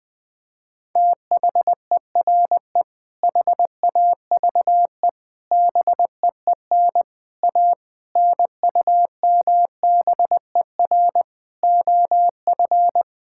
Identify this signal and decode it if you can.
Morse code, heard as a keyed tone, about 20 wpm, 700 Hz. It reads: THERE HAVE BEEN A NUMBER OF
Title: Sentences with 6-Word Phrases - Spoken Beginning - 20wpm